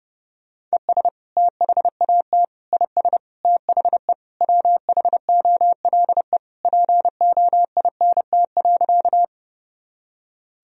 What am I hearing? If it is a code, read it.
Morse code, heard as a keyed tone, about 28 wpm, 700 Hz. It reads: ES THAT IS THE WHOLE POINT.
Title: Sentences from Top 400 Words and Abbreviations 30wpm